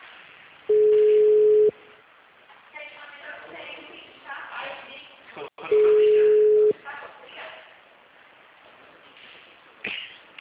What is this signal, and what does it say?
Ringback tone (Europe, 425 Hz), with speech.
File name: out- -2005- - - .95866